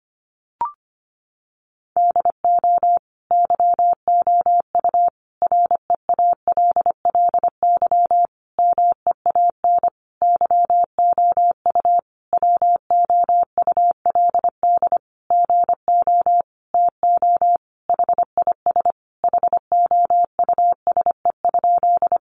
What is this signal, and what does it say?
Morse code, heard as a keyed tone, about 25 wpm, 700 Hz. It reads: DO YOU REALLY MEAN YOU WOULD GO TO HIS HOUSE?
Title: Sentences from Top 600 Words 25wpm